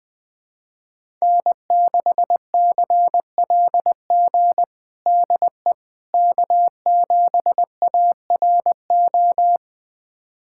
Morse code, heard as a keyed tone, about 21 wpm, 700 Hz. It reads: N6CLG DE K7ARO
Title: USA Callsign DE Callsign - 5 Characters - 20wpm